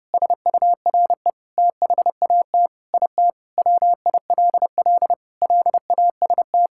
Morse code, heard as a keyed tone, about 30 wpm, 700 Hz. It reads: SURE THAT IT WILL LAST
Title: Sentences from Top 1600 Words 30wpm